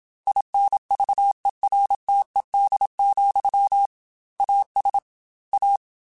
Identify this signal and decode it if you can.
Morse code, heard as a keyed tone, about 27 wpm, 800 Hz. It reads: INVERTED, AS A
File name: Technical Subjects / Technical Subjects-mp3s / XC @25